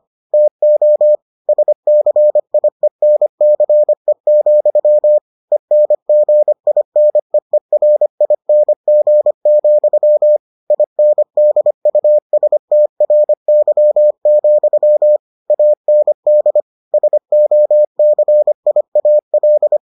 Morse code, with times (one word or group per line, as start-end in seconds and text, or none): 0.3-1.2 TO
1.5-5.2 SCIENCE,
5.5-10.4 ENGINEERING,
10.7-15.2 INDUSTRY,
15.5-16.6 AND
16.9-19.8 SOCIAL